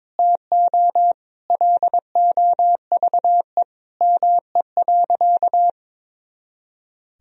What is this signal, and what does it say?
Morse code, heard as a keyed tone, about 21 wpm, 700 Hz. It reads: TO LOVE ME.
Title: Sentences from Top 400 Words and Abbreviations 22wpm